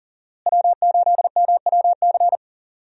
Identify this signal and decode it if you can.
Morse code, heard as a keyed tone, about 39 wpm, 700 Hz. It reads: W8MWC
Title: USA Call Signs - 5 Characters - Repeated 2x - 40wpm